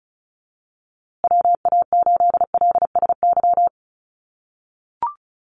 Morse code, heard as a keyed tone, about 34 wpm, 700 Hz. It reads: WA8LSY
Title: USA Call Signs Repeated 3x 35wpm